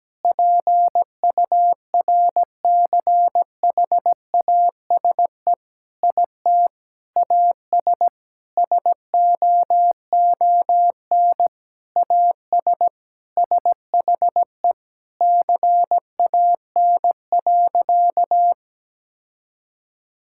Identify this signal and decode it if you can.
Morse code, heard as a keyed tone, about 17 wpm, 700 Hz. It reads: PURCHASE IT AS SOON AS SHE CAN.
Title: Sentences from Top 1000 Words - Review Part 5 - 17wpm